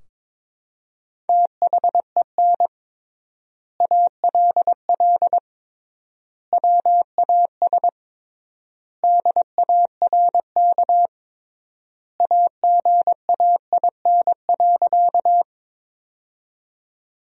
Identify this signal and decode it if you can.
Morse code, heard as a keyed tone, about 21 wpm, 700 Hz. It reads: THEN ALL WAS DARK AGAIN.
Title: 3x Word-Spacing - Sentences from Top 500 Words - 22wpm